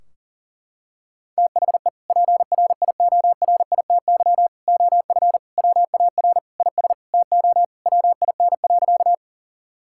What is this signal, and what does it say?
Morse code, heard as a keyed tone, about 40 wpm, 700 Hz. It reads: THE PRIORITY OF WAR IS TO WIN.
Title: Sentences from Top 1600 Words 40wpm